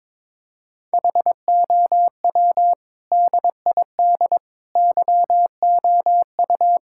Morse code, heard as a keyed tone, about 22 wpm, 700 Hz. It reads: HOW DID YOU
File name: practice / 22wpm / Sentences with 8-Word Phrases - 22wpm